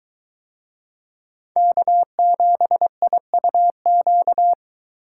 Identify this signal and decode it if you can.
Morse code, heard as a keyed tone, about 22 wpm, 700 Hz. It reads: K7IUQ